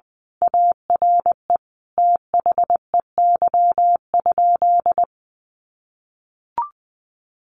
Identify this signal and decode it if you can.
Morse code, heard as a keyed tone, about 20 wpm, 700 Hz. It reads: ARE THEY?
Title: Sentences from Top 1200 Words 20wpm